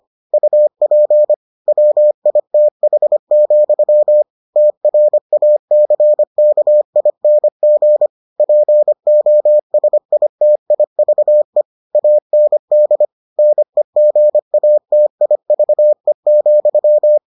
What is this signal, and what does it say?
Morse code, heard as a keyed tone, about 25 wpm, 600 Hz. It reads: UP WITH, TRACKING POSITIVE AND NEGATIVE,